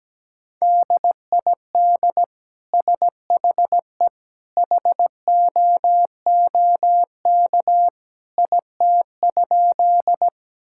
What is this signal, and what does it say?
Morse code, heard as a keyed tone, about 17 wpm, 700 Hz. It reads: DID SHE HOOK IT?
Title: Sentences from Top 1600 Words 17wpm